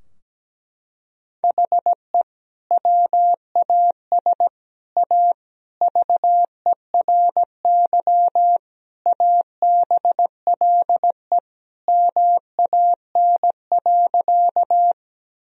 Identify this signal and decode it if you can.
Morse code, heard as a keyed tone, about 17 wpm, 700 Hz. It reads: HE WAS A VERY ABLE MAN.
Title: Sentences from Top 1000 Words - Review Part 2 - 17wpm